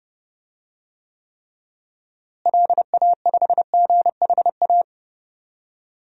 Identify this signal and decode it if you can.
Morse code, heard as a keyed tone, about 32 wpm, 700 Hz. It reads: LA5GHA
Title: DXCC-100 Call Signs - 6 Characters - Repeated 3x - 30wpm